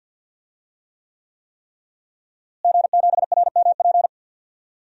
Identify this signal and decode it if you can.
Morse code, heard as a keyed tone, about 45 wpm, 700 Hz. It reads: G7RMP
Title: DXCC-100 Call Signs - 5 Characters - Repeated 3x - 50wpm